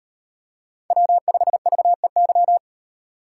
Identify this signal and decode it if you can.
Morse code, heard as a keyed tone, about 38 wpm, 700 Hz. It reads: W5VEY